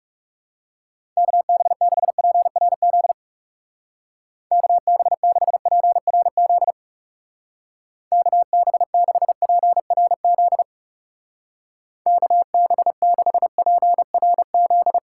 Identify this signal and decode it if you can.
Morse code, heard as a keyed tone, about 37 wpm, 700 Hz. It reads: KB6PRZ KB6PRZ KB6PRZ KB6PRZ
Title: Speed Racing - USA 2x3 Call Signs 30wpm